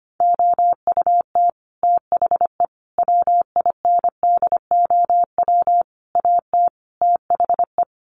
Morse code, heard as a keyed tone, about 25 wpm, 700 Hz. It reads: OUT THE WINDOW AT THE
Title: Sets of 5 Words Encore2 25wpm